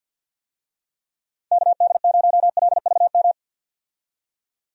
Morse code, heard as a keyed tone, about 51 wpm, 700 Hz. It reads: KD0LVM
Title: USA Call Signs - 6 Characters - 50wpm